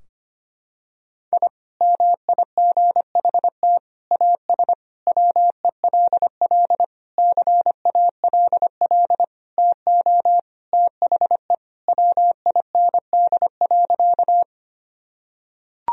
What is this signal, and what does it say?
Morse code, heard as a keyed tone, about 25 wpm, 700 Hz. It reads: I MIGHT AS WELL CALL TO THE WIND.
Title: Sentences from Top 1000 Words - Review Part 2 - 25wpm